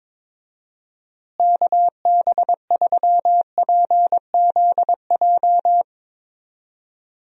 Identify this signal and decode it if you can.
Morse code, heard as a keyed tone, about 22 wpm, 700 Hz. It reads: KB3PZJ
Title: USA Call Signs - 6 Characters - Repeated 2x - 22wpm